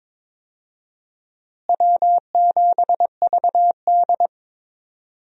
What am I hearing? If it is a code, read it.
Morse code, heard as a keyed tone, about 22 wpm, 700 Hz. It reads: W7VD
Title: USA Call Signs - 4 Characters - Repeated 2x - 22wpm